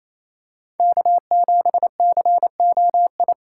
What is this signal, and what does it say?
Morse code, heard as a keyed tone, about 28 wpm, 700 Hz. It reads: K7COI
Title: USA Call Signs - 3x Character Spacing - 28wpm